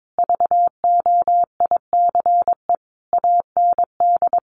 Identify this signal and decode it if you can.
Morse code, heard as a keyed tone, about 22 wpm, 700 Hz. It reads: VOICE AND
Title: Sentences from Top 500 Words 22wpm